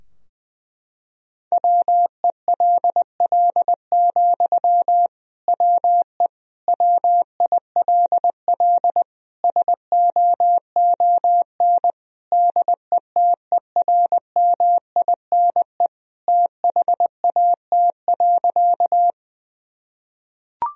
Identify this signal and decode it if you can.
Morse code, heard as a keyed tone, about 19 wpm, 700 Hz. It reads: WELL, WE WILL SOON DETERMINE THAT.
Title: Sentences from Top 1000 Words - Review Part 4 - 20wpm